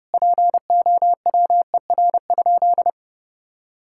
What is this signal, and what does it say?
Morse code, heard as a keyed tone, about 32 wpm, 700 Hz. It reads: POWER?
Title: Sentences from Top 1000 Words - Review Part 6 - 30wpm